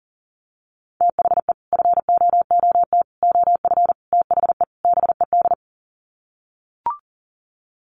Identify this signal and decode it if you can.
Morse code, heard as a keyed tone, about 40 wpm, 700 Hz. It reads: THE FOOT OF THE BED
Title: Sets of 5 Words Encore2 40wpm